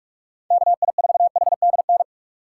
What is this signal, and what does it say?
Morse code, heard as a keyed tone, about 47 wpm, 700 Hz. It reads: KI4HDN
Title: USA Call Signs - 6x Character Spacing - 45wpm